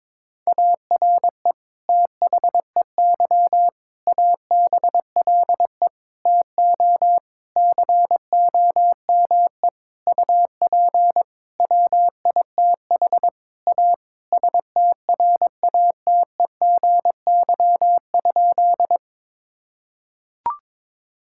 Morse code, heard as a keyed tone, about 22 wpm, 700 Hz. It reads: ARE THEY ABLE TO COME UP WITH A STRATEGY?
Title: Sentences from Top 1000 Words - Review Part 5 - 22wpm